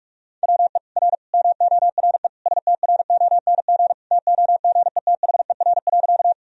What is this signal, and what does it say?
Morse code, heard as a keyed tone, about 46 wpm, 700 Hz. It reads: WE R MORE STRONG TOGETHER.
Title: Sentences from Top 500 Words and Abbreviations 45wpm